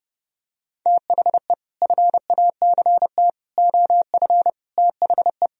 Morse code, heard as a keyed tone, about 30 wpm, 700 Hz. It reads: THE FACT OF THE